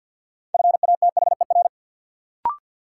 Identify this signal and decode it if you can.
Morse code, heard as a keyed tone, about 50 wpm, 700 Hz. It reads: FATHER
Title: Sentences from Top 300 Words - New Word Drill - 50wpm